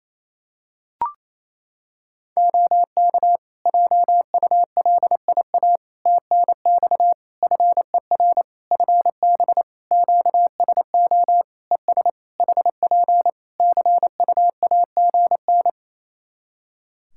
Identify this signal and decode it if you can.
Morse code, heard as a keyed tone, about 28 wpm, 700 Hz. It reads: OK JULIA TNX FER FB QSO ES HP CUAGN
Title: Practice QSO - Exchange 4 - Ending - 28wpm